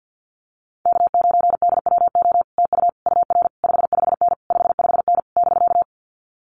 Morse code, heard as a keyed tone, about 50 wpm, 700 Hz. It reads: K9DWO TU UR 55N 55N <BK>
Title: Common QSO Phrases Encore 50wpm